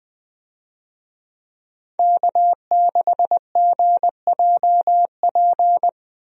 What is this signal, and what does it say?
Morse code, heard as a keyed tone, about 20 wpm, 700 Hz. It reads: K6GJP
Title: USA Call Signs - 5 Characters - Repeated 2x - 20wpm